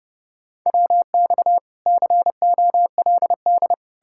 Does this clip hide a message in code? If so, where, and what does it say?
Yes: Morse code from 0.7 to 3.7 s, heard as a keyed tone, about 40 wpm, 700 Hz. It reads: WX COLD